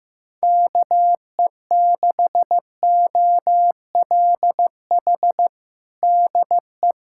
Morse code, heard as a keyed tone, about 15 wpm, 700 Hz. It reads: KE6OLH DE
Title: USA Callsign DE Callsign - 6 Characters - 15wpm